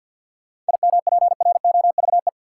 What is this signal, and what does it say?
Morse code, heard as a keyed tone, about 47 wpm, 700 Hz. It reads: IMPROVE
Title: Sentences from Top 800 Words - New Word Drill - 50wpm